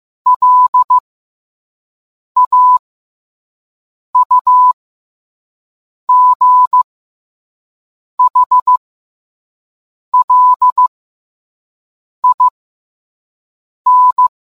Morse code, heard as a keyed tone, about 15 wpm, spaced as for 5 wpm, 1000 Hz. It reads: LAUGHLIN